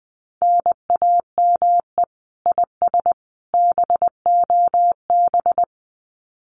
Morse code, heard as a keyed tone, about 20 wpm, 700 Hz. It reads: NAME IS BOB
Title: Common QSO Phrases 20wpm